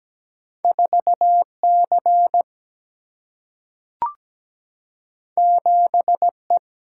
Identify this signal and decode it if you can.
Morse code, heard as a keyed tone, about 17 wpm, 700 Hz. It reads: 4 C 7 E
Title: Single-Letter Single-Number 17wpm